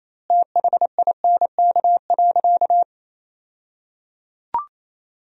Morse code, heard as a keyed tone, about 28 wpm, 700 Hz. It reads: THINK.
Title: 2x Word-Spacing - Sentences from Top 300 Words - 28wpm